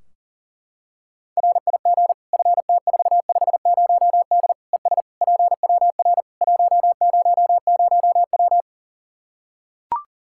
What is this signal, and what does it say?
Morse code, heard as a keyed tone, about 40 wpm, 700 Hz. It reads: RIG FT450D ES PWR 100W
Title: Practice QSO - Exchange 2 - Info - 40wpm